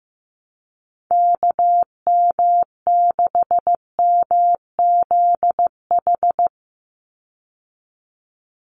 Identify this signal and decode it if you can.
Morse code, heard as a keyed tone, about 15 wpm, 700 Hz. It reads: KM6MZH
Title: USA Call Signs 15wpm